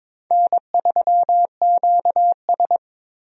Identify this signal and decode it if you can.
Morse code, heard as a keyed tone, about 22 wpm, 700 Hz. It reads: N3QS
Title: USA Call Signs - 4 Characters - Repeated 3x - 22wpm